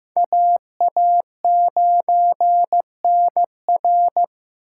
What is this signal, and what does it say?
Morse code, heard as a keyed tone, about 15 wpm, 700 Hz. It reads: AA9NR
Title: USA Call Signs - 4x Character Spacing - 15wpm